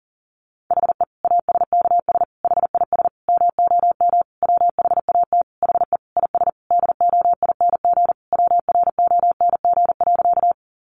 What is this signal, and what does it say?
Morse code, heard as a keyed tone, about 40 wpm, 700 Hz. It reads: HE ASKS HIS MOM WHAT HE IS DOING WRONG.
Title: Easy Phrases 40wpm